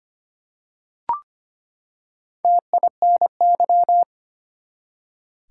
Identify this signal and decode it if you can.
Morse code, heard as a keyed tone, about 25 wpm, 700 Hz. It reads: TINY
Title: Four Letter Words - 25wpm